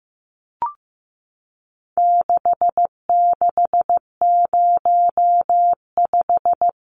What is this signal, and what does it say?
Morse code, heard as a keyed tone, about 14 wpm, 700 Hz. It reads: 6605
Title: Numbers - 4-Digit 15wpm